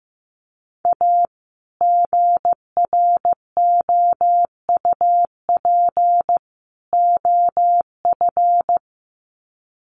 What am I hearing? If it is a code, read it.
Morse code, heard as a keyed tone, about 15 wpm, 700 Hz. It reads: A GROUP OF
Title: Sets of 3 Words 15wpm